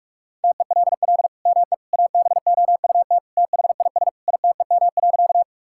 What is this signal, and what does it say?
Morse code, heard as a keyed tone, about 45 wpm, 700 Hz. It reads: TELL ME ABOUT THIS ITEM.
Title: Sentences from Top 1000 Words - Review Part 6 - 45wpm